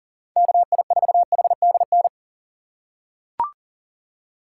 Morse code, heard as a keyed tone, about 37 wpm, 700 Hz. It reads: KI4HDN
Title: USA Call Signs - 6 Characters - 6x Character Spacing - 40wpm